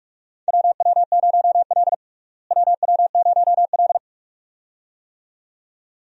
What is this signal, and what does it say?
Morse code, heard as a keyed tone, about 42 wpm, 700 Hz. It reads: WW0L WW0L
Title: USA Call Signs - 4 Characters - Repeated 2x - 45wpm